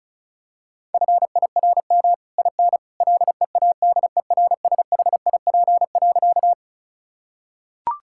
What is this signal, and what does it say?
Morse code, heard as a keyed tone, about 36 wpm, 700 Hz. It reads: FIRM IN LEADERSHIP.
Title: Sentences from Top 1600 Words 35wpm